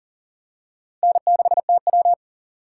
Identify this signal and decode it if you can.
Morse code, heard as a keyed tone, about 40 wpm, 700 Hz. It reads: N6TW
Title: K1USN SST QSO Practice - 40wpm